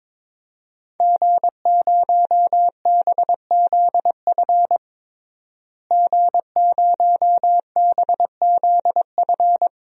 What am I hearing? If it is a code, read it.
Morse code, heard as a keyed tone, about 22 wpm, 700 Hz. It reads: G0BZF G0BZF
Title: DXCC-100 Call Signs - 5 Characters - Repeated 2x - 22wpm